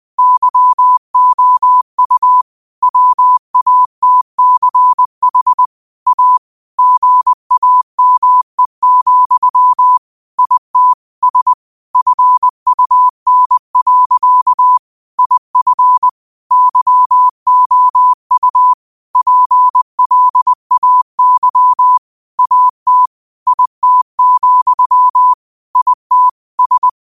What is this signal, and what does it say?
Morse code, heard as a keyed tone, about 20 wpm, 1000 Hz. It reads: YOU WATCH A GAME, IT S FUN. IF YOU PLAY AT IT, IT S
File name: qotdfiles / 20wpm